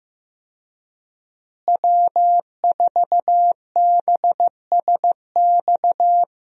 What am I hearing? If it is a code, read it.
Morse code, heard as a keyed tone, about 15 wpm, 700 Hz. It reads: W4BSX
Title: USA Call Signs - 5 Characters - Repeated 2x - 15wpm